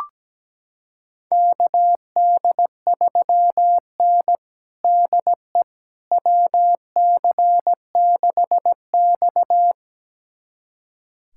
Morse code, heard as a keyed tone, about 17 wpm, 700 Hz. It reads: KD3N DE WC6X
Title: USA Callsign DE Callsign - 4 Characters - 17wpm